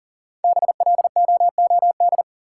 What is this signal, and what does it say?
Morse code, heard as a keyed tone, about 40 wpm, 700 Hz. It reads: BLOOD
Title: Sentences from Top 1000 Words - Review Part 2 - New Word Drill - 40wpm